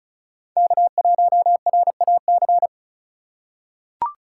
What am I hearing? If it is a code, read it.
Morse code, heard as a keyed tone, about 36 wpm, 700 Hz. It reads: K1RAC
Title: USA Call Signs - 5 Characters - 3x Character Spacing - 35wpm